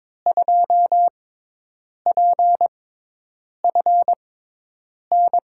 Morse code, heard as a keyed tone, about 22 wpm, spaced as for 8 wpm, 700 Hz. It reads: 2PFN
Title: USA Call Signs - 6 Characters - 6x Character Spacing - 22wpm